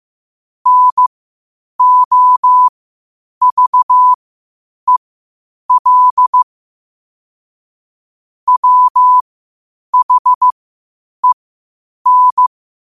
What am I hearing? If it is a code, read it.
Morse code, heard as a keyed tone, about 15 wpm, spaced as for 8 wpm, 1000 Hz. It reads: NOVEL WHEN
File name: qotdfiles / 8wpm